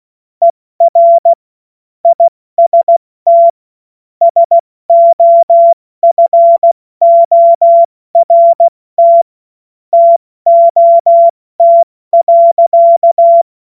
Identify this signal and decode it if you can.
Morse code, heard as a keyed tone, about 16 wpm, 680 Hz. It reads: ER IST SOFORT TOT.
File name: Morsekurs / Witze / Witz 30 Januar 2024 70 F80